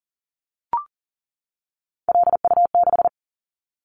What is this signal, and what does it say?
Morse code, heard as a keyed tone, about 40 wpm, 700 Hz. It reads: LU6
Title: Callsign Prefixes for Top 50 DXCC - 40wpm